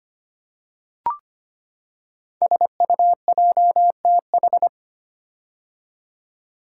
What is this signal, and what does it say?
Morse code, heard as a keyed tone, about 25 wpm, 700 Hz. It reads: SUJTH
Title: Code Groups - 5 Letter:Numbers - 25wpm